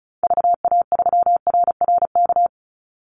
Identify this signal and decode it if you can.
Morse code, heard as a keyed tone, about 34 wpm, 700 Hz. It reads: VA3RRK
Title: Call Signs - 35wpm